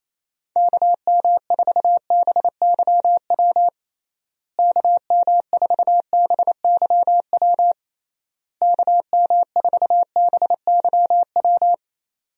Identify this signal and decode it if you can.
Morse code, heard as a keyed tone, about 28 wpm, 700 Hz. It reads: KM4BYW KM4BYW KM4BYW
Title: USA Call Signs - 6 Characters - Repeated 3x - 28wpm